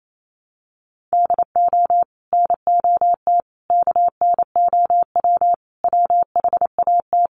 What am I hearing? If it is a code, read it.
Morse code, heard as a keyed tone, about 28 wpm, 700 Hz. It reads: DO NOT KNOW WHAT
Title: Sets of 5 Words Encore2 28wpm